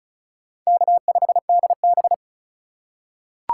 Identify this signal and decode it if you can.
Morse code, heard as a keyed tone, about 36 wpm, 700 Hz. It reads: K5DB
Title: USA Call Signs - 4 Characters - 2x Character Spacing - 35wpm